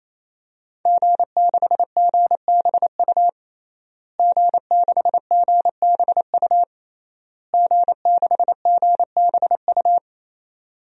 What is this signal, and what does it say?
Morse code, heard as a keyed tone, about 28 wpm, 700 Hz. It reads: G6GBU G6GBU G6GBU
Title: DXCC-100 Call Signs - 5 Characters - Repeated 3x - 28wpm